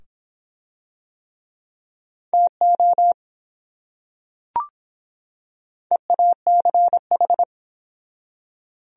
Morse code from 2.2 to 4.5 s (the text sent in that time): TO